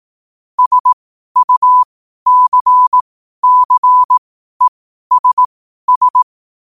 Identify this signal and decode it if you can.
Morse code, heard as a keyed tone, about 18 wpm, spaced as for 12 wpm, 1000 Hz. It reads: SUCCESS
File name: qotdfiles / 12wpm